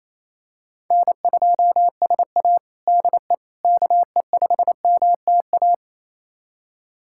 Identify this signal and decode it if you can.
Morse code, heard as a keyed tone, about 28 wpm, 700 Hz. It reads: N2SA DE KE5MTA
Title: USA Callsign DE Callsign 28wpm